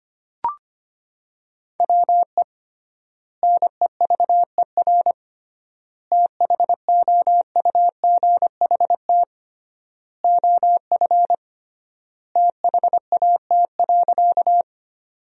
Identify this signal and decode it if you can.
Morse code, heard as a keyed tone, about 25 wpm, 700 Hz. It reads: WE NEVER THOUGHT OF THAT.
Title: 3x Word-Spacing - Sentences from Top 300 Words - 25wpm